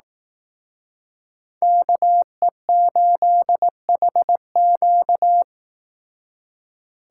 Morse code, heard as a keyed tone, about 18 wpm, 700 Hz. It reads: KE8HQ
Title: Speed Racing - USA Call Signs 15wpm